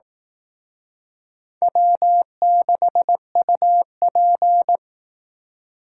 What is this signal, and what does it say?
Morse code, heard as a keyed tone, about 18 wpm, 700 Hz. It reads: W6UP
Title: Speed Racing - USA 1x2 Call Signs 15wpm